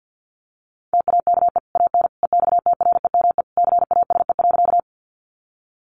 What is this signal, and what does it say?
Morse code, heard as a keyed tone, about 48 wpm, 700 Hz. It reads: TAKE AN EXTREME CASE.
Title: Sentences from Top 1300 Words 50wpm